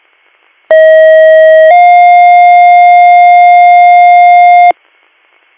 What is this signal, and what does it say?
Two-tone page, heard as a one-second tone, then a three-second tone, about 640 then 710 Hz. It is heard from 0.7 to 4.7 s.